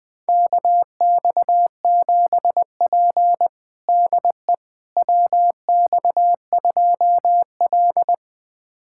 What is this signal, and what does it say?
Morse code, heard as a keyed tone, about 20 wpm, 700 Hz. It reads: KX7P DE WX2L
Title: USA Callsign DE Callsign - 4 Characters - 20wpm